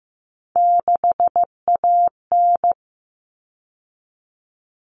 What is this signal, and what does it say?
Morse code, heard as a keyed tone, about 15 wpm, 700 Hz. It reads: 6AN
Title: Common Cut Numbers - 3-Digit 15wpm